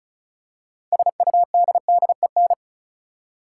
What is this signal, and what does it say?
Morse code, heard as a keyed tone, about 36 wpm, 700 Hz. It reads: SUDDEN
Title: Six Letter Words - 35wpm